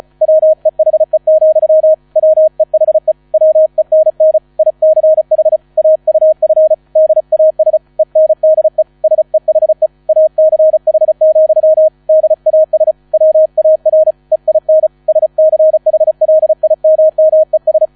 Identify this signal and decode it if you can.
Morse code, heard as a keyed tone, about 34 wpm, 630 Hz. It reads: WEHE, WEHE WENN ICH AUF DAS ENDE SEHE ACH, DAS WAR EIN SCHLIMMES